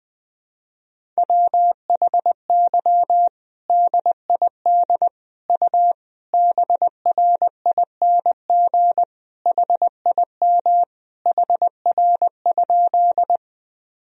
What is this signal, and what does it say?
Morse code, heard as a keyed tone, about 20 wpm, 700 Hz. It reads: WHY DID U BRING HIM HR?
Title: Sentences from Top 500 Words and Abbreviations 20wpm